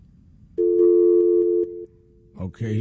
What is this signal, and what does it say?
Dial tone (North America), 350 + 440 Hz.